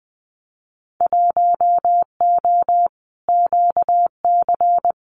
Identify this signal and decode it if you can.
Morse code, heard as a keyed tone, about 20 wpm, 700 Hz. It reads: 1O QC
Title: Winter Field Day Exchanges 20wpm